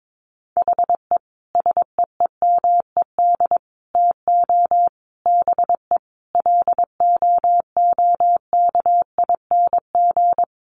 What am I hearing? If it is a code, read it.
Morse code, heard as a keyed tone, about 22 wpm, 700 Hz. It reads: HE SEEMED TO BE LOOKING
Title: A Sound Of Thunder 22wpm